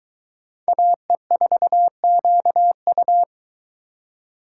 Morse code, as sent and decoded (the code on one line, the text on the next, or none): .- . ....- --.- ..-
AE4QU